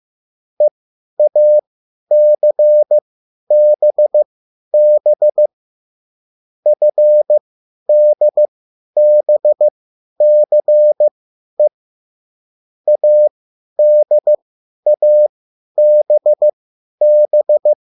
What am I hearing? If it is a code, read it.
Morse code, heard as a keyed tone, about 15 wpm, spaced as for 10 wpm, 600 Hz. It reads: EACBB FDBCE ADABB
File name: cwbook